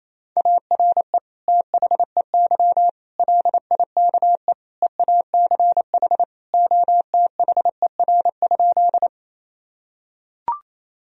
Morse code, heard as a keyed tone, about 28 wpm, 700 Hz. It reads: ARE THEY LIKE EACH OTHER?
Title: Sentences from Top 1000 Words - Review Part 1 - 28wpm